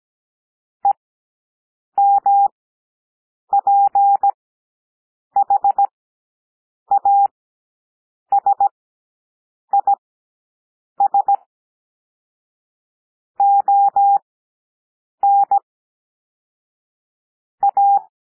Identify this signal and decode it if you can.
Morse code, heard as a keyed tone, about 17 wpm, spaced as for 7 wpm, 800 Hz. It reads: EMPHASIS ON A
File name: SOUND PLAIN 17 wpm